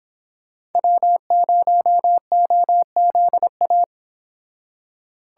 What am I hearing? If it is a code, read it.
Morse code, heard as a keyed tone, about 32 wpm, 700 Hz. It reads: W0OZA